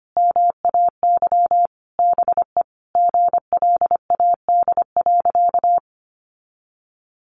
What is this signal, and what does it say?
Morse code, heard as a keyed tone, about 25 wpm, 700 Hz. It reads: MAY BE GLAD.
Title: Sentences from Top 800 Words 25wpm